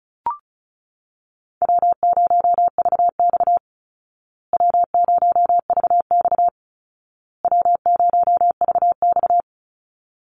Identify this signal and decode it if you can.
Morse code, heard as a keyed tone, about 34 wpm, 700 Hz. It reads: W0VX W0VX W0VX
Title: Call Signs Repeated 2x - 35wpm